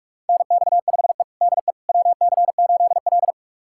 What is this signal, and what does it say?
Morse code, heard as a keyed tone, about 46 wpm, 700 Hz. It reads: NX5E DE WC8L
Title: USA Callsign DE Callsign - 4 Characters - 45wpm